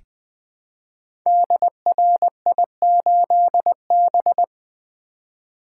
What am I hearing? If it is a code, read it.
Morse code, heard as a keyed tone, about 20 wpm, 700 Hz. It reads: DRI8B
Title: Code Groups - 5 Letter:Numbers - 20wpm